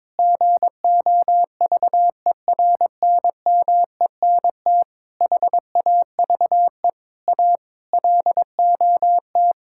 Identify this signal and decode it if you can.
Morse code, heard as a keyed tone, about 21 wpm, 700 Hz. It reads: GOVERNMENT HAVE A LOT
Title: Sentences from Top 1000 Words - Review Part 5 - 22wpm